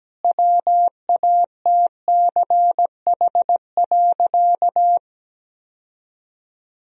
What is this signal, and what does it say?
Morse code, heard as a keyed tone, about 17 wpm, 700 Hz. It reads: WATCH.
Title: Sentences from Top 1000 Words - Review Part 1 - 17wpm